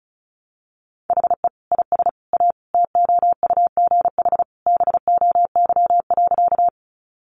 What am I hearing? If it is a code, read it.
Morse code, heard as a keyed tone, about 34 wpm, 700 Hz. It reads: HE IS A TOUGH BOY.
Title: Sentences from Top 1200 Words 35wpm